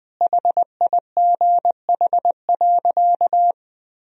Morse code, heard as a keyed tone, about 21 wpm, 700 Hz. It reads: HIGH.
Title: Sentences from Top 1000 Words - Review Part 1 - 20wpm